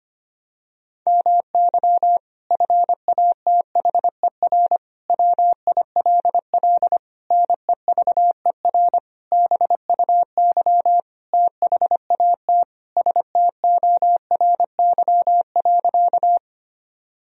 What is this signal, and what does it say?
Morse code, heard as a keyed tone, about 25 wpm, 690 Hz. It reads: MY FATHER WILL NEVER BUY THAT STORY.
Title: Sentences from Top 1000 Words - Review Part 4 - 25wpm